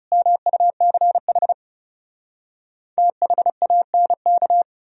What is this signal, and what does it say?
Morse code, heard as a keyed tone, about 37 wpm, 700 Hz. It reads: MUCH THANK
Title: Speed-Racing - Sets of 5 Words 30wpm